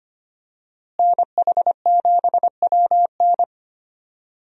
Morse code, heard as a keyed tone, about 25 wpm, 700 Hz. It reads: NH7WN
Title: USA Call Signs - 5 Characters - Repeated 2x - 25wpm